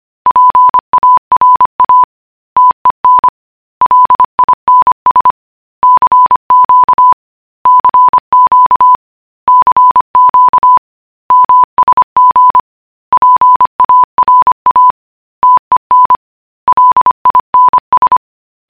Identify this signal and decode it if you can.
Morse code, heard as a keyed tone, about 25 wpm, 1000 Hz. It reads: PARA TEN LINS CQ CQ CQ MSG PARA TEN LINS